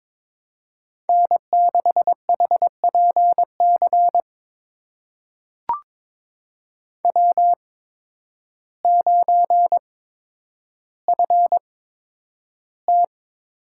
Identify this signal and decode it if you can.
Morse code, heard as a keyed tone, about 22 wpm, spaced as for 10 wpm, 700 Hz. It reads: N6HPC W9FT
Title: USA Call Signs - 5 Characters - 8x Character Spacing - 22wpm